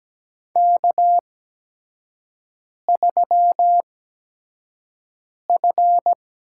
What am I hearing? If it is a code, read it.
Morse code, heard as a keyed tone, about 17 wpm, spaced as for 5 wpm, 700 Hz. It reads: K3F
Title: USA Call Signs - 4 Characters - 8x Character Spacing - 17wpm